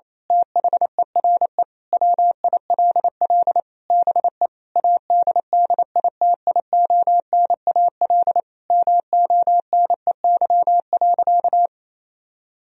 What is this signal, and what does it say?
Morse code, heard as a keyed tone, about 28 wpm, 700 Hz. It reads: THERE WILL BE ADDITIONAL MONEY.
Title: Sentences from Top 1000 Words - Review Part 4 - 28wpm